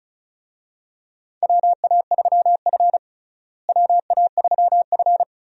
Morse code, heard as a keyed tone, about 36 wpm, 700 Hz. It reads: WA3F WA3F
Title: USA Call Signs - 4 Characters - Repeated 3x - 35wpm